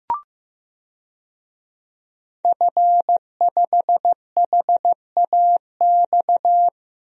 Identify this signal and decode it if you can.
Morse code, heard as a keyed tone, about 15 wpm, 700 Hz. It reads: F5HAX
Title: DXCC-100 Call Signs - 5 Characters - 15wpm